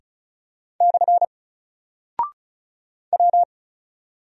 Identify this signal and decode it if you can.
Morse code, heard as a keyed tone, about 36 wpm, 700 Hz. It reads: / W
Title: Single Character (Letter:Number:Punctuation:Prosign) Alt 35wpm